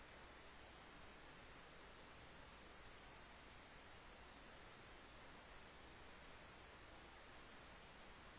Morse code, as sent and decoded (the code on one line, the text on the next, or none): none